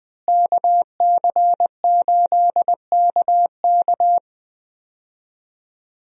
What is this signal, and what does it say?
Morse code, heard as a keyed tone, about 20 wpm, 700 Hz. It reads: KC8KK
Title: USA Call Signs - 5 Characters - Repeated 2x - 20wpm